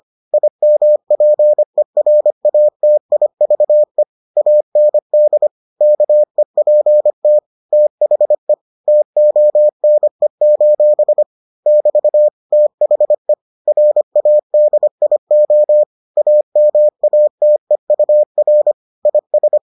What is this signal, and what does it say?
Morse code, heard as a keyed tone, about 25 wpm, 600 Hz. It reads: IMPERATIVE AND KEPT THE TONE: <BT> THE RADIO AMATEUR IS